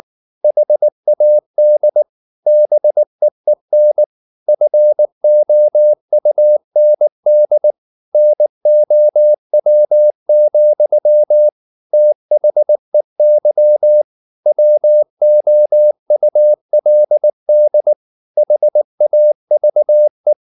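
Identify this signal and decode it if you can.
Morse code, heard as a keyed tone, about 19 wpm, 600 Hz. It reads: HAD BEEN FOUND NOW, THEY WOULD HAVE
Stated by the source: Morse code at 19 wpm from CW med Gnister, reading Ken Follett's novel Never in English.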